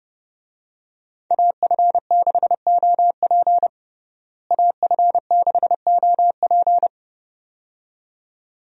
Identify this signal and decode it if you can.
Morse code, heard as a keyed tone, about 30 wpm, 700 Hz. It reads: AF6OP AF6OP
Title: USA Call Signs - 5 Characters - Repeated 2x - 30wpm